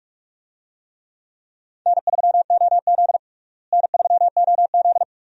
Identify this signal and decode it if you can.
Morse code, heard as a keyed tone, about 43 wpm, 700 Hz. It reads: N3OZ N3OZ
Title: USA Call Signs - 4 Characters - Repeated 2x - 45wpm